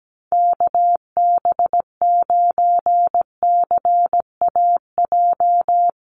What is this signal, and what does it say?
Morse code, heard as a keyed tone, about 17 wpm, 700 Hz. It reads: KB9CAJ
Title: USA Call Signs Repeated 2x 17wpm